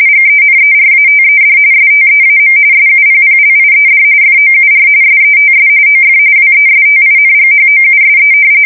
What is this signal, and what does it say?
RTTY, about 45.45 baud. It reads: GENERATED STEAM SUPPLIED FOUR SETS OF WESTINGHOUSE
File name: fsk170 / battleshipnj